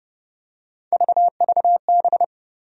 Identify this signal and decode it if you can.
Morse code, heard as a keyed tone, about 30 wpm, 700 Hz. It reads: VVB
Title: USA Call Sign Postfixes - 3 Letters 30wpm